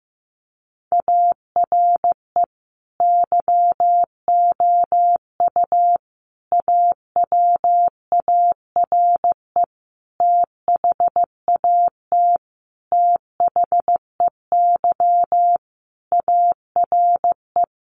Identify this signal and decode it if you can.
Morse code, heard as a keyed tone, about 15 wpm, 700 Hz. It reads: ARE YOU AWARE THAT THEY ARE
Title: Sentences from Top 1200 Words 15wpm